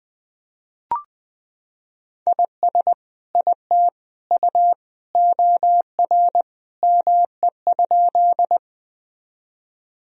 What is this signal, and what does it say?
Morse code, heard as a keyed tone, about 20 wpm, 700 Hz. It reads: IS IT U OR ME?
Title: Sentences from Top 200 Words and Abbreviations 20wpm